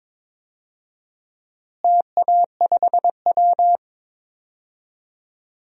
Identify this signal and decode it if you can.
Morse code, heard as a keyed tone, about 22 wpm, 700 Hz. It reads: TA5W
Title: DXCC-100 Call Signs - 4 Characters - Repeated 2x - 22wpm